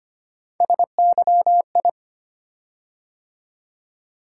Morse code, heard as a keyed tone, about 25 wpm, 700 Hz. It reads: SYI
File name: Three Letter-Number Combinations - 25wpm